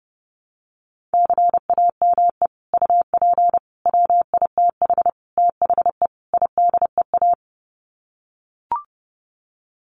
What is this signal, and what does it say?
Morse code, heard as a keyed tone, about 30 wpm, 700 Hz. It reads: CAME UP WITH THE IDEA
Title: Sets of 5 Words Encore2 30wpm